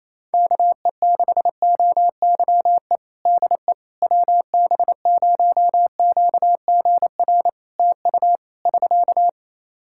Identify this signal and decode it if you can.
Morse code, heard as a keyed tone, about 27 wpm, 700 Hz. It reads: KE6OYE DE WB0QGR TU <SK>
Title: Practice QSO - Exchange 4 - Ending - 28wpm